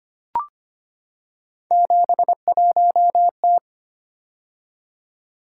Morse code, heard as a keyed tone, about 25 wpm, 700 Hz. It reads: 71T
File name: Three Letter-Number Combinations - 25wpm